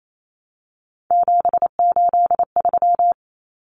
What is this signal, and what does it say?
Morse code, heard as a keyed tone, about 28 wpm, 700 Hz. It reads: 783
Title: Common Cut Numbers - 3-Digit 28wpm